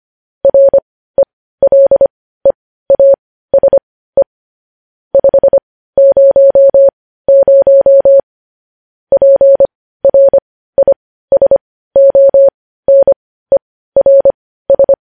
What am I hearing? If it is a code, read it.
Morse code, heard as a keyed tone, about 25 wpm, spaced as for 15 wpm, 550 Hz. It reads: RELEASE 500 PRISONERS
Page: news-headlines-in-morse-code-at-15-wpm